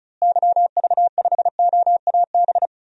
Morse code, heard as a keyed tone, about 36 wpm, 700 Hz. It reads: YV5OAB
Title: DXCC-100 Call Signs - 6 Characters - Repeated 3x - 35wpm